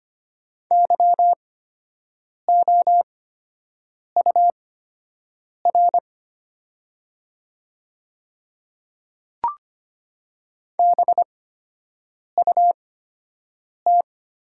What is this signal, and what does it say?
Morse code, heard as a keyed tone, about 25 wpm, spaced as for 7 wpm, 700 Hz. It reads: YOUR BUT
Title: 8x Character Spacing - Top 100 Words - 25wpm